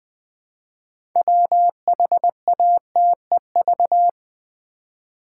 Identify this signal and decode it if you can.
Morse code, heard as a keyed tone, about 20 wpm, 700 Hz. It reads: WHATEV
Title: Build A Word - 3 Letters At A Time - 8-Letter Words - 20wpm